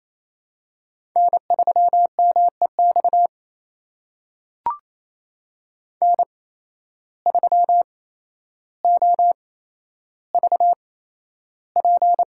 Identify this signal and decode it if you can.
Morse code, heard as a keyed tone, about 28 wpm, spaced as for 12 wpm, 700 Hz. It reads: N3MEX N3OVP
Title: USA Call Signs - 5 Characters - 8x Character Spacing - 28wpm